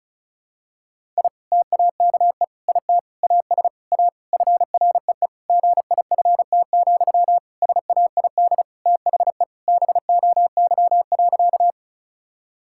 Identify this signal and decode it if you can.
Morse code, heard as a keyed tone, about 36 wpm, 700 Hz. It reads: I TAKE IT AS A FREE GIFT, SAID THE BOY.
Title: Sentences from Top 1000 Words - Review Part 6 - 35wpm